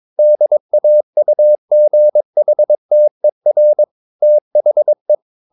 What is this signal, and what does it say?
Morse code, heard as a keyed tone, about 22 wpm, 600 Hz. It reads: DAUGHTER THE
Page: Romanen Never af Ken Follet på engelsk.